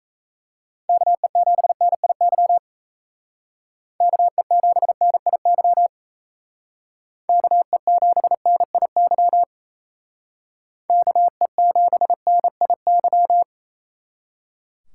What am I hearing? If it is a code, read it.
Morse code, heard as a keyed tone, about 33 wpm, 700 Hz. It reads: KE7NIY KE7NIY KE7NIY KE7NIY
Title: Speed Racing - USA 2x3 Call Signs 28wpm